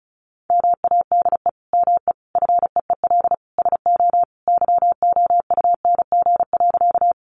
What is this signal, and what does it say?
Morse code, heard as a keyed tone, about 34 wpm, 700 Hz. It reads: MADE ME FEEL SO YOUNG.
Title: Sentences from Top 400 Words 35wpm